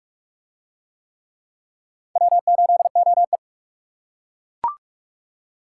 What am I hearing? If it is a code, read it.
Morse code, heard as a keyed tone, about 40 wpm, 700 Hz. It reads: W8OE